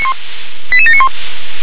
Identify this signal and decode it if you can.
Five-tone selective call (ZVEI), heard as quick runs of five tones, tones of about 70 ms. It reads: 88791
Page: SDR/ZVEI